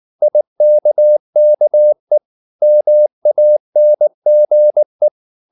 Morse code, heard as a keyed tone, about 19 wpm, 600 Hz. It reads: IKKE MANGE
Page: Ferie 19 wpm